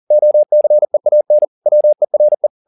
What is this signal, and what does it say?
Morse code, heard as a keyed tone, about 40 wpm, 600 Hz. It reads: OCEAN WERE
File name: the old man and the sea 2 40wpm